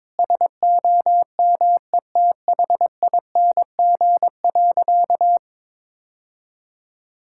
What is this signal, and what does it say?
Morse code, heard as a keyed tone, about 23 wpm, 700 Hz. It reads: SOMETHING.
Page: morse-code-ninja-practice-22wpm